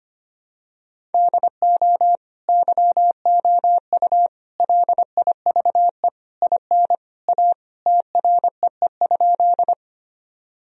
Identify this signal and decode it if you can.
Morse code, heard as a keyed tone, about 25 wpm, 700 Hz. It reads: DO YOU LIVE IN A TREE?
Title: Sentences from Top 1000 Words - Review Part 1 - 25wpm